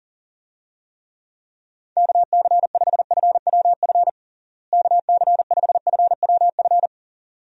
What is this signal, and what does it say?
Morse code, heard as a keyed tone, about 37 wpm, 700 Hz. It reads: KC5FWF KC5FWF
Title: USA Call Signs - 6 Characters - Repeated 2x - 40wpm